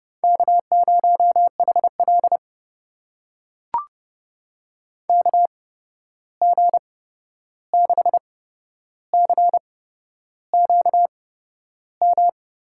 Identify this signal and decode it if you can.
Morse code, heard as a keyed tone, about 30 wpm, spaced as for 11 wpm, 700 Hz. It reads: K0HL KG6CQM
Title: USA Call Signs - 8x Character Spacing - 30wpm